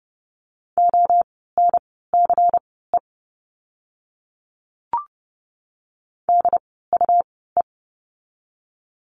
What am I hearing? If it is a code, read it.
Morse code, heard as a keyed tone, about 30 wpm, spaced as for 17 wpm, 700 Hz. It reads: ONCE DUE
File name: Lesson 17f - Letter u - Words 30wpm